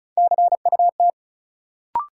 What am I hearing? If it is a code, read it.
Morse code, heard as a keyed tone, about 34 wpm, 700 Hz. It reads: CUT
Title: Sentences from Top 300 Words - New Word Drill - 35wpm